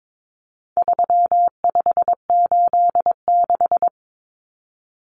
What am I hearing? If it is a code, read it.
Morse code, heard as a keyed tone, about 22 wpm, 700 Hz. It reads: 3586
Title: Numbers - 4-Digit 22wpm